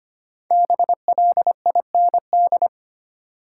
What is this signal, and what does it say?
Morse code, heard as a keyed tone, about 25 wpm, 700 Hz. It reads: BLIND
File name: 5 Letter Words - First 3 Letters Given - 25wpm